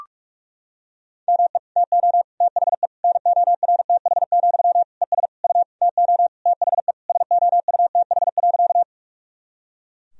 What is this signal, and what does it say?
Morse code, heard as a keyed tone, about 41 wpm, 700 Hz. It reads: ME TO THE NORTH, ES U TO THE SOUTH.
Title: Sentences from Top 400 Words and Abbreviations 45wpm